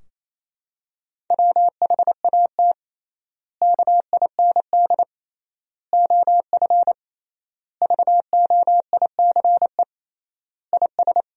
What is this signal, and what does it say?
Morse code, heard as a keyed tone, about 28 wpm, 700 Hz. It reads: WHAT KIND OF VOICE IS
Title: 3x Word-Spacing - Sentences from Top 500 Words - 28wpm